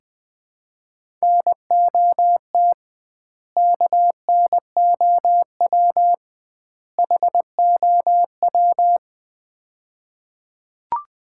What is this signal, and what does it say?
Morse code, heard as a keyed tone, about 20 wpm, 700 Hz. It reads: NOT KNOW HOW
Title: 2x Word-Spacing - Sets of 3 Words - 20wpm